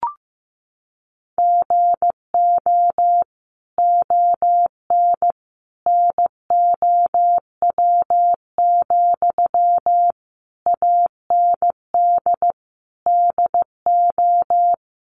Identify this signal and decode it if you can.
Morse code, heard as a keyed tone, about 15 wpm, 700 Hz. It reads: GO ON NOW, AND DO
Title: Sentences from Top 1500 Words 15wpm